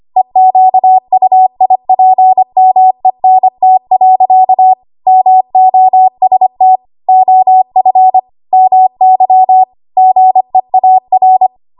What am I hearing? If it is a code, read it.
Morse code, heard as a keyed tone, about 25 wpm, 750 Hz. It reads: EQUIPMENT. MOST OF MY GEAR
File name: Archive / 25 WPM / 25WPM